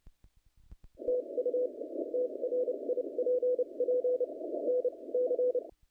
Morse code, heard as a keyed tone, about 29 wpm, spaced as for 24 wpm, 500 Hz. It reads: TU TRIPP NC
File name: contests / naqp-cw-k5tr-op-k5pi- -0100